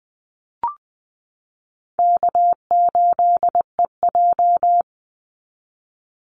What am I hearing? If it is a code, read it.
Morse code, heard as a keyed tone, about 20 wpm, 700 Hz. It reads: K8EJ
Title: Easy Call Signs Repeated 3x - 20wpm